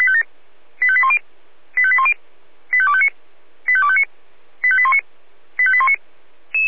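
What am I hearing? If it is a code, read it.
Five-tone selective call (ZVEI), heard as quick runs of five tones, tones of about 70 ms. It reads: 85810 85810 85259 85259 85819 85819
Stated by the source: ZVEI five-tone selective call received by SDR.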